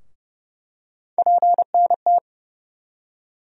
Morse code, heard as a keyed tone, about 30 wpm, 700 Hz. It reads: PNT
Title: USA Call Sign Postfixes - 3 Letters 30wpm